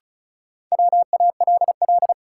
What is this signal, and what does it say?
Morse code, heard as a keyed tone, about 36 wpm, 700 Hz. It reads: WALL
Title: Sentences from Top 1000 Words - Review Part 6 - New Word Drill - 35wpm